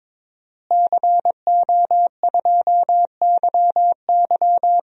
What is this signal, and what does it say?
Morse code, heard as a keyed tone, about 22 wpm, 700 Hz. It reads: CO2YY
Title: DXCC-100 Call Signs - 5 Characters - Repeated 3x - 22wpm